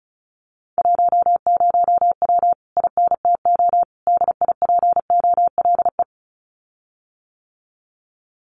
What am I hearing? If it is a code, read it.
Morse code, heard as a keyed tone, about 34 wpm, 700 Hz. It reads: 10W INTO DIPOLE
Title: Common QSO Phrases Encore 35wpm